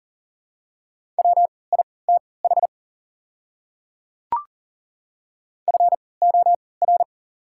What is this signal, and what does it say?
Morse code, heard as a keyed tone, about 40 wpm, spaced as for 23 wpm, 690 Hz. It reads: WITH FOR
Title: 3x Character Spacing - Top 100 Words - 40wpm